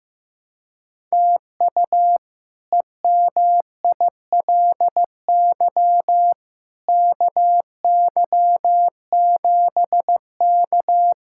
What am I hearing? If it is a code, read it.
Morse code, heard as a keyed tone, about 15 wpm, 700 Hz. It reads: TU EMILY KY7K
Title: K1USN SST QSO Practice - 15wpm